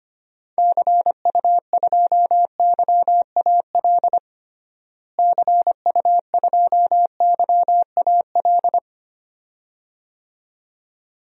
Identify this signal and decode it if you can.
Morse code, heard as a keyed tone, about 25 wpm, 700 Hz. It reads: CU2YAL CU2YAL
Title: DXCC-100 Call Signs - 6 Characters - Repeated 2x - 25wpm